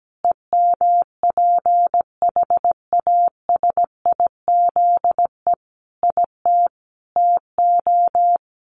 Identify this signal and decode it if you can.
Morse code, heard as a keyed tone, about 17 wpm, 700 Hz. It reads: EMPHASIZE IT TO
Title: Sentences from Top 1500 Words 17wpm